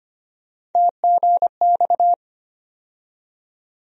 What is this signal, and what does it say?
Morse code, heard as a keyed tone, about 25 wpm, 700 Hz. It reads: TGX
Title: USA Call Sign Postfixes - 3 Letters 25wpm